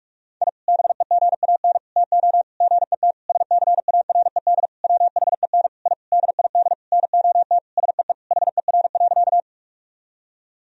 Morse code, heard as a keyed tone, about 46 wpm, 700 Hz. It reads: I BEGAN TO GET SCARED WHEN I DID NOT SEE HER.
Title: Sentences from Top 1000 Words - Review Part 4 - 45wpm